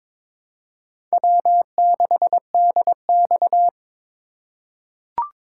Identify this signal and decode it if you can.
Morse code, heard as a keyed tone, about 22 wpm, 700 Hz. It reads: W6DX
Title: USA Call Signs - 4 Characters - 2x Character Spacing - 22wpm